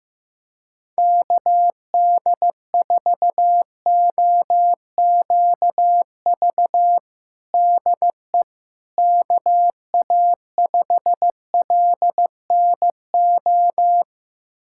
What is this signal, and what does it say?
Morse code, heard as a keyed tone, about 15 wpm, 700 Hz. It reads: KD4OQV DE KA5LNO
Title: USA Callsign DE Callsign - 6 Characters - 15wpm